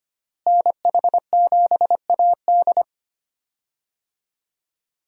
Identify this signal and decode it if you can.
Morse code, heard as a keyed tone, about 25 wpm, 700 Hz. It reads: NH7AD
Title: USA Call Signs - 5 Characters - Repeated 3x - 25wpm